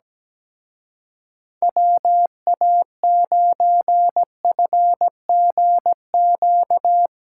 Morse code, heard as a keyed tone, about 18 wpm, 700 Hz. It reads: WA9FGQ
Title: Speed Racing - USA 2x3 Call Signs 17wpm